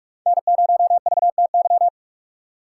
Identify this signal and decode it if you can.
Morse code, heard as a keyed tone, about 42 wpm, 700 Hz. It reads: N0VTY